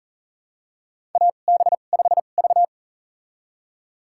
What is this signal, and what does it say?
Morse code, heard as a keyed tone, about 37 wpm, spaced as for 28 wpm, 690 Hz. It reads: AB5V
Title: USA Call Signs - 4 Characters - 2x Character Spacing - 40wpm